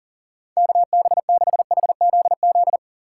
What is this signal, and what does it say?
Morse code, heard as a keyed tone, about 48 wpm, 700 Hz. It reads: KB6HZZ
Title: Speed Racing - USA 2x3 Call Signs 40wpm